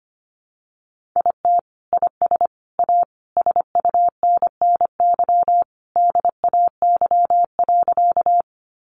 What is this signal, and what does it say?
Morse code, heard as a keyed tone, about 25 wpm, 700 Hz. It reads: IT IS A SUNNY DAY.
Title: Additional Easy Phrases 25wpm